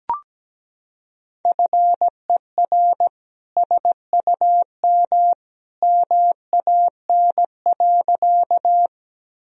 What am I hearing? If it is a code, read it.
Morse code, heard as a keyed tone, about 17 wpm, 700 Hz. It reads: FER SUM MAN.
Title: Sentences from Top 200 Words and Abbreviations 17wpm